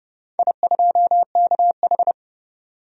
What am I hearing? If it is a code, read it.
Morse code, heard as a keyed tone, about 30 wpm, 700 Hz. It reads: I2KH
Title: DXCC-100 Call Signs - 4 Characters - 30wpm